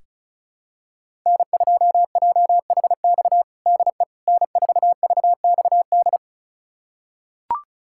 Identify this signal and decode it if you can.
Morse code, heard as a keyed tone, about 36 wpm, 700 Hz. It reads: N2JHX DE N4VXD
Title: USA Callsign DE Callsign - 5 Characters - 35wpm